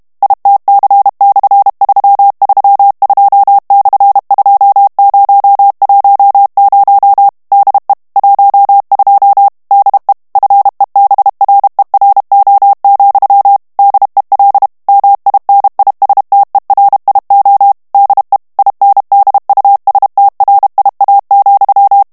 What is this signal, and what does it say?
Morse code, heard as a keyed tone, about 32 wpm, 800 Hz. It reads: ITC/332/2010 DE 12 DE FEBRERO, DEL MINISTERIO DE INDUSTRIA,
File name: ESPAGNOL TEXTE EN CLAIR 2 VITESSE 30 MOTS MINUTES